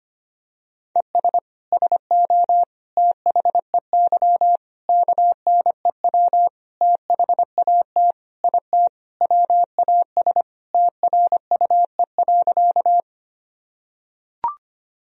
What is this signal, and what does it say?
Morse code, heard as a keyed tone, about 25 wpm, 700 Hz. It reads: ES SO THEY KNEW THAT IT WAS TRUE.
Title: Sentences from Top 400 Words and Abbreviations 25wpm